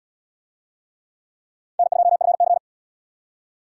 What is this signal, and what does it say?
Morse code, heard as a keyed tone, about 72 wpm, 700 Hz. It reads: N2FC